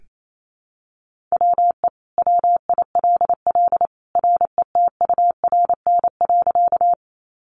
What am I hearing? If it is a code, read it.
Morse code, heard as a keyed tone, about 28 wpm, 700 Hz. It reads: WE WILL RETURN.
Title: Sentences from Top 800 Words 28wpm